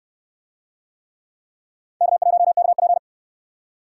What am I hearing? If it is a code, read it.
Morse code, heard as a keyed tone, about 61 wpm, 700 Hz. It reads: K1KP